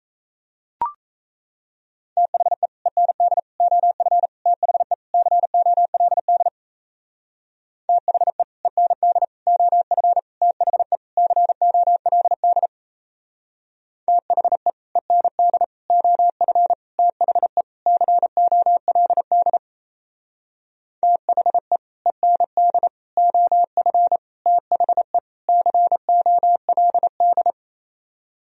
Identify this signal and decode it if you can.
Morse code, heard as a keyed tone, about 35 wpm, 700 Hz. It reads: THE END OF THE COLD THE END OF THE COLD THE END OF THE COLD THE END OF THE COLD